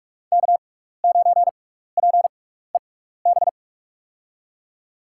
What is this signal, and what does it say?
Morse code, heard as a keyed tone, about 45 wpm, spaced as for 15 wpm, 700 Hz. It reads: K9PEB
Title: USA Call Signs - 5 Characters - 6x Character Spacing - 45wpm